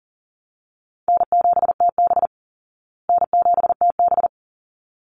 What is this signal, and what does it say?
Morse code, heard as a keyed tone, about 40 wpm, 700 Hz. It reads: N7TB N7TB
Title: Call Signs Repeated 3x - 40wpm